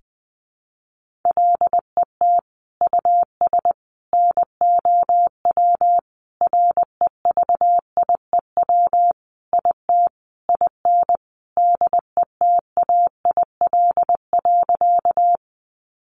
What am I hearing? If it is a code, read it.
Morse code, heard as a keyed tone, about 20 wpm, 700 Hz. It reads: LET US NOW REVIEW IT IN DETAIL.
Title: Sentences from Top 1500 Words 20wpm